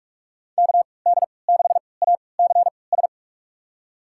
Morse code, heard as a keyed tone, about 44 wpm, spaced as for 25 wpm, 700 Hz. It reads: KD6ACS
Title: USA Call Signs - 6 Characters - 3x Character Spacing - 45wpm